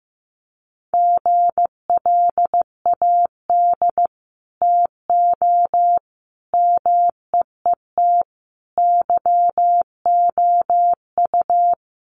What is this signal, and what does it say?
Morse code, heard as a keyed tone, about 15 wpm, 700 Hz. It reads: GLAD TO MEET YOU
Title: Sentences from Top 1100 Words 15wpm